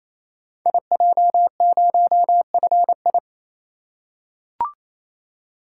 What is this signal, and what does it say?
Morse code, heard as a keyed tone, about 29 wpm, 700 Hz. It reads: IJ0FI
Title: Code Groups - 5 Letter:Numbers - 28wpm